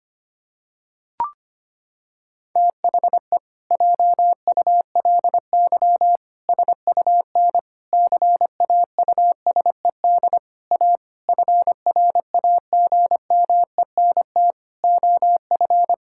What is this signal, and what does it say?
Morse code, heard as a keyed tone, about 25 wpm, 700 Hz. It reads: THE JULY SUN CAUSED A FRAGMENT OF
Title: Pangram Challenge v2 - 25wpm